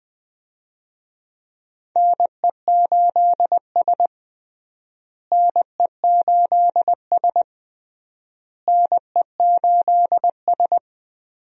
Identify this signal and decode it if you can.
Morse code, heard as a keyed tone, about 20 wpm, 700 Hz. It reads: NE8S NE8S NE8S
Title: USA Call Signs - 4 Characters - Repeated 3x - 20wpm